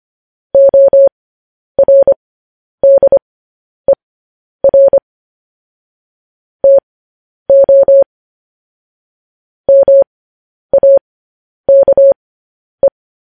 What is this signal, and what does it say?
Morse code, heard as a keyed tone, about 25 wpm, spaced as for 10 wpm, 550 Hz. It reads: ORDER TO MAKE
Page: news headlines in morse code at 10 wpm